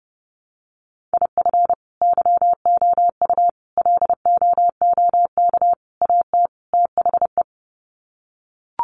Morse code, heard as a keyed tone, about 30 wpm, 700 Hz. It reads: IF YOU LOOK AT THE
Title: Sets of 5 Words 30wpm